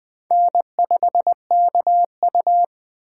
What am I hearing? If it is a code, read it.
Morse code, heard as a keyed tone, about 20 wpm, 700 Hz. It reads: N5KU
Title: Speed Racing - USA Call Signs 17wpm